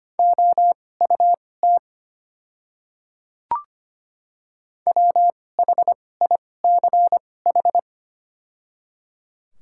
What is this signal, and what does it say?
Morse code, heard as a keyed tone, about 25 wpm, spaced as for 18 wpm, 700 Hz. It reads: OUT WHICH
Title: 2x Character Spacing - Top 500 Words - 25wpm